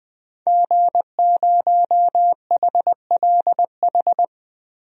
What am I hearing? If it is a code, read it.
Morse code, heard as a keyed tone, about 21 wpm, 700 Hz. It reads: G0HLH
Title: DXCC-100 Call Signs - 5 Characters - Repeated 3x - 20wpm